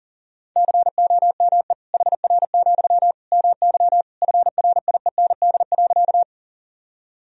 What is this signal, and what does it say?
Morse code, heard as a keyed tone, about 40 wpm, 700 Hz. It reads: COME HR, MY FRIEND.
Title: Sentences from Top 300 Words and Abbreviations 40wpm